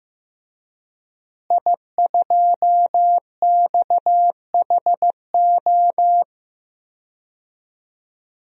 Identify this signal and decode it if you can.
Morse code, heard as a keyed tone, about 15 wpm, 700 Hz. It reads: I2XHO
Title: DXCC-100 Call Signs - 5 Characters - Repeated 3x - 15wpm